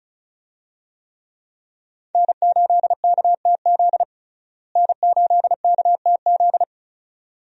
Morse code, heard as a keyed tone, about 36 wpm, 700 Hz. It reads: N8KTZ N8KTZ
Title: USA Call Signs - 5 Characters - Repeated 2x - 35wpm